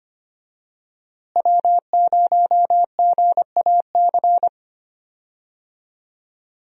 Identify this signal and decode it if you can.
Morse code, heard as a keyed tone, about 25 wpm, 700 Hz. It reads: W0GAC
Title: USA Call Signs - 5 Characters - Repeated 2x - 25wpm